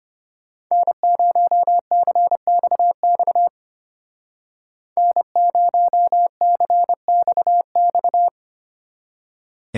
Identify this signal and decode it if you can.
Morse code, heard as a keyed tone, about 32 wpm, 700 Hz. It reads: N0CXX N0CXX